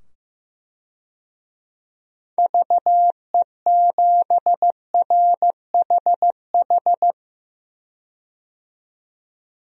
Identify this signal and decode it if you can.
Morse code, heard as a keyed tone, about 15 wpm, 700 Hz. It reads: VE7RHH
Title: DXCC-100 Call Signs - 6 Characters - 15wpm